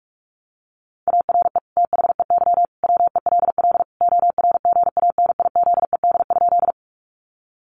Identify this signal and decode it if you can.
Morse code, heard as a keyed tone, about 44 wpm, 700 Hz. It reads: ARE THEY WELL ORGANIZED?
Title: Sentences from Top 1300 Words 45wpm